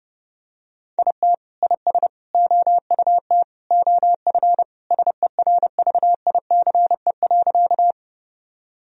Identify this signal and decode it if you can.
Morse code, heard as a keyed tone, about 30 wpm, 700 Hz. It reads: IT IS OUT OF SERVICE.
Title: Sentences from Top 1000 Words - Review Part 4 - 30wpm